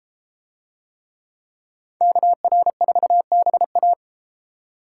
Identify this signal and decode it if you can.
Morse code, heard as a keyed tone, about 33 wpm, 700 Hz. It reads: KR4BA